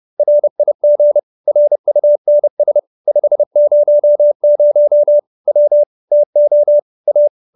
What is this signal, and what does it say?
Morse code, heard as a keyed tone, about 30 wpm, 590 Hz. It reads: RIG RUNS 500 W TO A
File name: QSO315 30